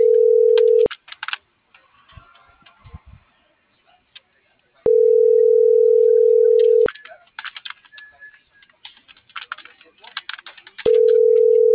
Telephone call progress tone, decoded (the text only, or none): ringback tone